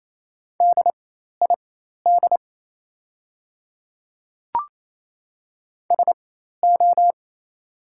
Morse code, heard as a keyed tone, about 28 wpm, spaced as for 13 wpm, 700 Hz. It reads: DID SO
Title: 4x Character Spacing - Top 100 Words - 28wpm